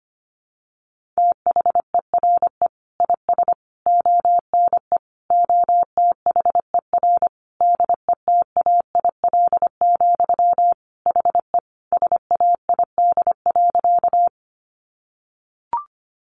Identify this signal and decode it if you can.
Morse code, heard as a keyed tone, about 25 wpm, 700 Hz. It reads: THERE IS ONE OTHER DETAIL, HE SAID.
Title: Sentences from Top 1500 Words 25wpm